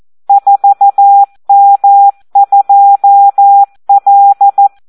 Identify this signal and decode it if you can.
Morse code, heard as a keyed tone, about 14 wpm, 800 Hz. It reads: 4M2L
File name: Audio practica CW / SD 1 14